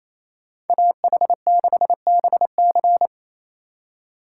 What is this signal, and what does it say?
Morse code, heard as a keyed tone, about 28 wpm, 700 Hz. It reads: AH6BC
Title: USA Call Signs - 2x Character Spacing - 28wpm